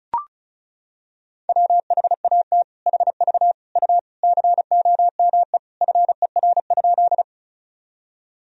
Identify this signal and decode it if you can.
Morse code, heard as a keyed tone, about 36 wpm, 700 Hz. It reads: WHAT HV U COME FER?
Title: Sentences from Top 100 Words and Abbreviations 35wpm